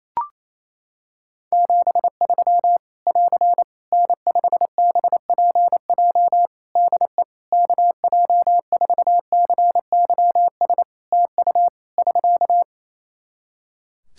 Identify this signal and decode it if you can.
Morse code, heard as a keyed tone, about 27 wpm, 700 Hz. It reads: 73 <AR> N5BPJ DE KJ4CYS TU <SK>
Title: Practice QSO - Exchange 4 - Ending - 28wpm